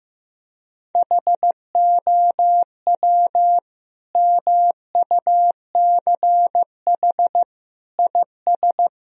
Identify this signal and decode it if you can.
Morse code, heard as a keyed tone, about 16 wpm, 700 Hz. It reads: HOW MUCH IS
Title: Sentences from Top 1000 Words - Review Part 2 - 15wpm